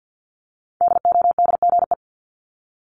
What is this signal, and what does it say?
Morse code, heard as a keyed tone, about 50 wpm, 700 Hz. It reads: DODGE
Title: Multi-letter Phonemes Repeated 3x 50wpm